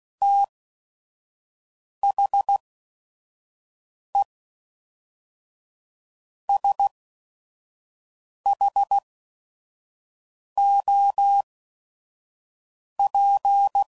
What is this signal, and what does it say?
Morse code, heard as a keyed tone, about 16 wpm, spaced as for 5 wpm, 800 Hz. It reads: THE SHOP
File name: CW Code Groups @5WPM / Weather Sets / Weather 4 24